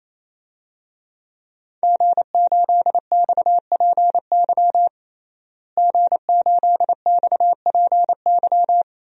Morse code, heard as a keyed tone, about 28 wpm, 700 Hz. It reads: G8XPY G8XPY
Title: DXCC-100 Call Signs - 5 Characters - Repeated 3x - 28wpm